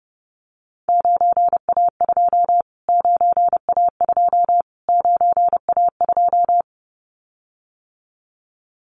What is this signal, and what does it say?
Morse code, heard as a keyed tone, about 30 wpm, 700 Hz. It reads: 9A2 9A2 9A2
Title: Callsign Prefixes for Top 25 DXCC - Repeated 3x - 30wpm